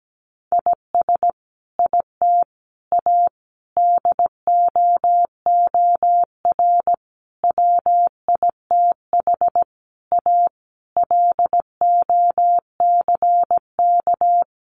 Morse code, heard as a keyed tone, about 17 wpm, 700 Hz. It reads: IS IT A DOOR WITH A LOCK
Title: Sentences from Top 1300 Words 17wpm